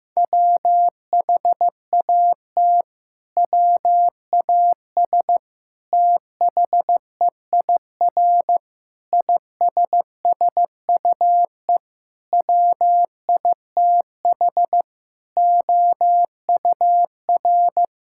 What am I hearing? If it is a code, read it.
Morse code, heard as a keyed tone, about 14 wpm, 700 Hz. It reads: WHAT WAS THEIR ISSUE WITH OUR
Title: Sentences from Top 1000 Words - Review Part 6 - 15wpm